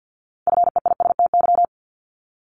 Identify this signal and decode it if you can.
Morse code, heard as a keyed tone, about 50 wpm, 700 Hz. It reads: FEISTY
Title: Multi-letter Phonemes 50wpm